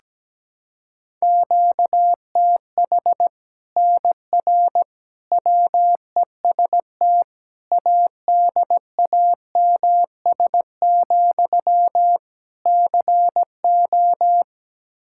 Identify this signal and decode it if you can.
Morse code, heard as a keyed tone, about 17 wpm, 700 Hz. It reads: QTH NR WEST ADAMS, CO
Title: Practice QSO - Exchange 1 - Beginning - 17wpm